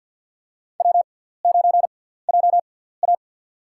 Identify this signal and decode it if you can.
Morse code, heard as a keyed tone, about 53 wpm, spaced as for 17 wpm, 700 Hz. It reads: W9JA
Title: USA Call Signs - 4 Characters - 6x Character Spacing - 50wpm